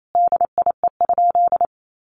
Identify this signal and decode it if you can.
Morse code, heard as a keyed tone, about 28 wpm, 700 Hz. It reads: DIE?
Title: Sentences from Top 1300 Words 28wpm